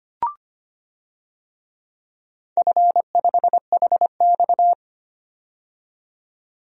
Morse code, heard as a keyed tone, about 25 wpm, 700 Hz. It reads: F5HX
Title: DXCC-100 Call Signs - 4 Characters - 25wpm